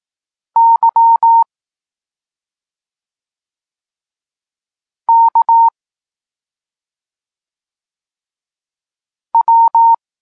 Morse code, heard as a keyed tone, about 18 wpm, 920 Hz. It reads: Y K W
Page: Tekst 32 – Doskonalenie odbioru poznanych liter